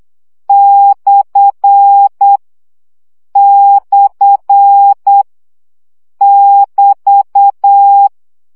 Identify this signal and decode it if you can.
Morse code, heard as a keyed tone, about 8 wpm, 800 Hz. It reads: / / <BT>
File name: CW gyakorlas per 7 baud